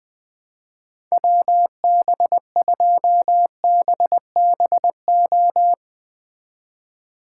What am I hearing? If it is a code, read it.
Morse code, heard as a keyed tone, about 20 wpm, 700 Hz. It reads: WB2BBO